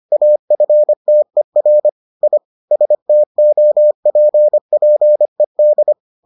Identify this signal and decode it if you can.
Morse code, heard as a keyed tone, about 25 wpm, 600 Hz. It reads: AFTER I STOPPED